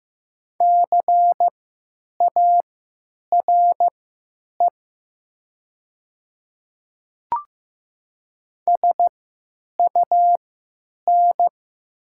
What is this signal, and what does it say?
Morse code, heard as a keyed tone, about 15 wpm, spaced as for 9 wpm, 690 Hz. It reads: CARE SUN
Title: 3x Character Spacing - Top 300 Words - 15wpm